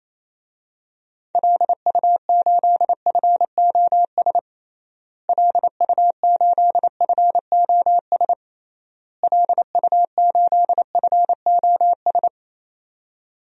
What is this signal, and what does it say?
Morse code, heard as a keyed tone, about 28 wpm, 700 Hz. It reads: LU8FOS LU8FOS LU8FOS
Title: DXCC-100 Call Signs - 6 Characters - Repeated 3x - 28wpm